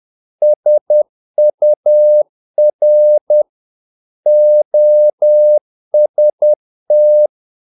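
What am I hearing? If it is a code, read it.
Morse code, heard as a keyed tone, about 10 wpm, 600 Hz. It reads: SUR OST
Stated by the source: Morse practice of short Danish words at 10 wpm, from CW med Gnister.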